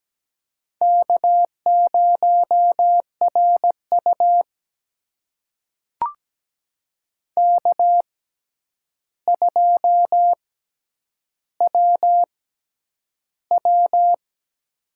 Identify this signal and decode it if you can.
Morse code, heard as a keyed tone, about 17 wpm, spaced as for 9 wpm, 700 Hz. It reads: K0RU K2WW
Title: USA Call Signs - 4 Characters - 6x Character Spacing - 17wpm